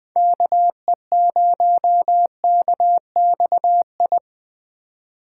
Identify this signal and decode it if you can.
Morse code, heard as a keyed tone, about 24 wpm, 700 Hz. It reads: KE0KXI
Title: Speed Racing - USA 2x3 Call Signs 20wpm